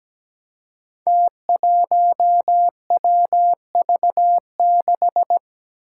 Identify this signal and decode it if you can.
Morse code, heard as a keyed tone, about 17 wpm, 700 Hz. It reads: T1WV6
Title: Five Challenging Letter-Number Combinations 17wpm